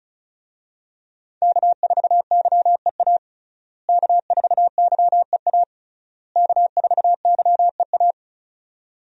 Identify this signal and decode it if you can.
Morse code, heard as a keyed tone, about 34 wpm, 700 Hz. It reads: K4YEA K4YEA K4YEA
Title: USA Call Signs - 5 Characters - Repeated 3x - 35wpm